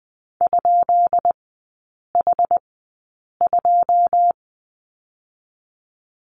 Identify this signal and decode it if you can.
Morse code, heard as a keyed tone, about 20 wpm, spaced as for 8 wpm, 700 Hz. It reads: ?H2
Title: 159 Lesson 26e - Letter g - Three Characters 20wpm